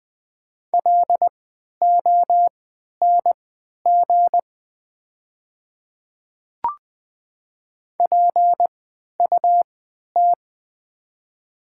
Morse code, heard as a keyed tone, about 20 wpm, spaced as for 11 wpm, 700 Hz. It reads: LONG PUT
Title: 3x Character Spacing - Top 300 Words - 20wpm